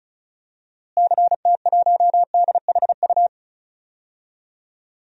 Morse code, heard as a keyed tone, about 36 wpm, 700 Hz. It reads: CT1DHU
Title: DXCC-100 Call Signs - 6 Characters - Repeated 3x - 35wpm